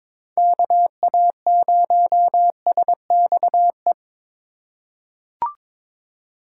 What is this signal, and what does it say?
Morse code, heard as a keyed tone, about 22 wpm, 700 Hz. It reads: KA0SXE
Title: USA Call Signs - 6x Character Spacing - 22wpm